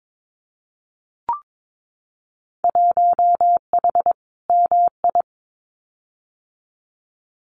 Morse code, heard as a keyed tone, about 22 wpm, 700 Hz. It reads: 1H MI
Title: Winter Field Day Exchanges 22wpm